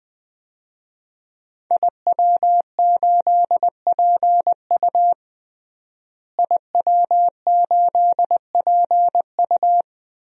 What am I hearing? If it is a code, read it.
Morse code, heard as a keyed tone, about 20 wpm, 700 Hz. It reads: IW8PU IW8PU
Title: DXCC-100 Call Signs - 5 Characters - Repeated 2x - 20wpm